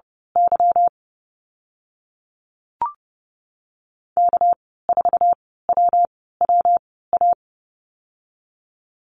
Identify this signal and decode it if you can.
Morse code, heard as a keyed tone, about 32 wpm, 700 Hz. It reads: Y K 4 W W A
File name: Lesson 37g - Letter x - Call Signs 30wpm